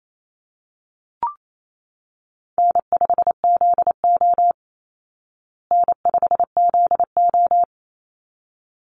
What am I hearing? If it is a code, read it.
Morse code, heard as a keyed tone, about 28 wpm, 700 Hz. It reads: N5ZO N5ZO
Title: Call Signs Repeated 3x - 28wpm